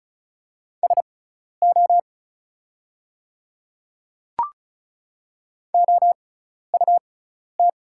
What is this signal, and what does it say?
Morse code, heard as a keyed tone, about 36 wpm, spaced as for 12 wpm, 700 Hz. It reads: SO OUT
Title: 6x Character Spacing - Top 100 Words - 35wpm